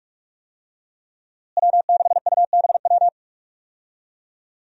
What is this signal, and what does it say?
Morse code, heard as a keyed tone, about 46 wpm, 700 Hz. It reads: W6UBW
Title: USA Call Signs - 5 Characters - Repeated 3x - 45wpm